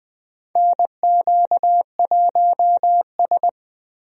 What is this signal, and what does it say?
Morse code, heard as a keyed tone, about 20 wpm, 700 Hz. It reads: NQ1S
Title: USA Call Signs - 4 Characters - Repeated 2x - 20wpm